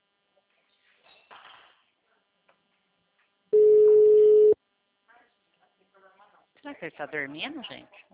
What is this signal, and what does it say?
Ringback tone (Europe, 425 Hz), with speech.